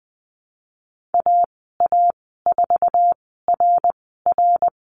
Morse code, heard as a keyed tone, about 20 wpm, spaced as for 15 wpm, 700 Hz. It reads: AA4RR
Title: 069 Lesson 12g - Letter d - Call Signs 20wpm